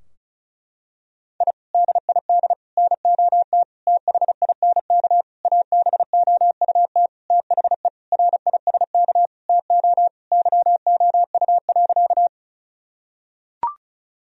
Morse code, heard as a keyed tone, about 36 wpm, 700 Hz. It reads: I DID NOT THINK ABOUT THE RISK TO YOU.
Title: Sentences from Top 1000 Words - Review Part 4 - 35wpm